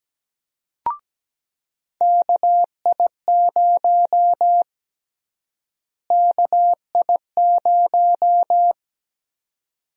Morse code, heard as a keyed tone, about 17 wpm, 700 Hz. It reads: KI0 KI0
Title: USA Call Sign Prefixes - Repeated 2x - 17wpm